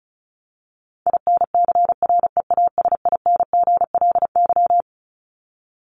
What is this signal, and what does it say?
Morse code, heard as a keyed tone, about 34 wpm, 700 Hz. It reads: INCREASINGLY
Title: LY Suffix - Top 100 Words with Base Words 35wpm